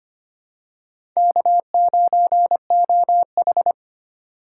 Morse code, heard as a keyed tone, about 25 wpm, 700 Hz. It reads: K9OH
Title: USA Call Signs - 2x Character Spacing - 25wpm